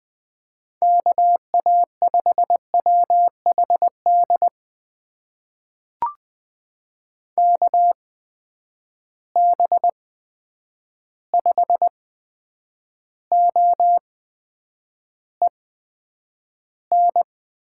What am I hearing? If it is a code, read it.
Morse code, heard as a keyed tone, about 20 wpm, spaced as for 9 wpm, 700 Hz. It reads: KA5WHD KB5OEN
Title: USA Call Signs - 6 Characters - 8x Character Spacing - 20wpm